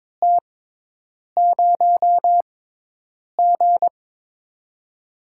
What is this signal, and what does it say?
Morse code, heard as a keyed tone, about 22 wpm, spaced as for 8 wpm, 700 Hz. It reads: T0G
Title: USA Call Signs - 6x Character Spacing - 22wpm